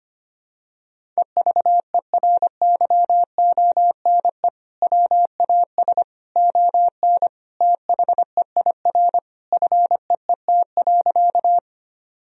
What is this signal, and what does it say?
Morse code, heard as a keyed tone, about 25 wpm, 700 Hz. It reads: EVERYONE WAS ON THEIR FEET.
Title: Sentences from Top 1000 Words - Review Part 1 - 25wpm